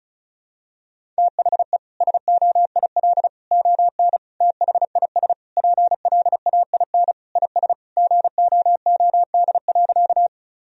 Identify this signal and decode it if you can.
Morse code, heard as a keyed tone, about 36 wpm, 700 Hz. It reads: THE SOIL ON THIS PLAIN IS GOOD.
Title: Sentences from Top 1000 Words - Review Part 4 - 35wpm